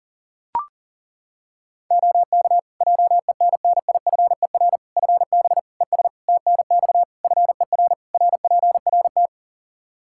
Morse code, heard as a keyed tone, about 40 wpm, 700 Hz. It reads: OK JENNIFER FB ES TNX FER RPRT
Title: Practice QSO - Exchange 2 - Info - 40wpm